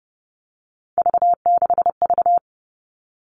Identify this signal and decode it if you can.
Morse code, heard as a keyed tone, about 30 wpm, 700 Hz. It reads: V6V
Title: All Cut Numbers - 3-Digit 30wpm